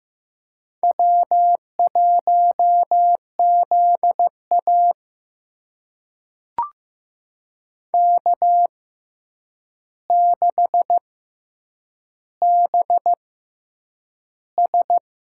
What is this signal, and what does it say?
Morse code, heard as a keyed tone, about 15 wpm, spaced as for 8 wpm, 700 Hz. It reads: W1ZA K6BS
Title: USA Call Signs - 4 Characters - 6x Character Spacing - 15wpm